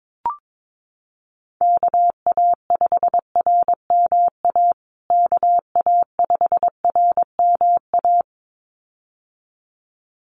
Morse code, heard as a keyed tone, about 22 wpm, 700 Hz. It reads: KA5RMA KA5RMA
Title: USA Call Signs Repeated 2x 22wpm